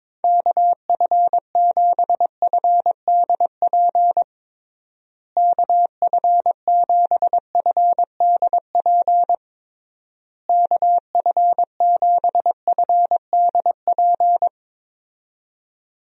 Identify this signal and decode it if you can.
Morse code, heard as a keyed tone, about 22 wpm, 700 Hz. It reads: KF7FDP KF7FDP KF7FDP
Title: USA Call Signs - 6 Characters - Repeated 3x - 22wpm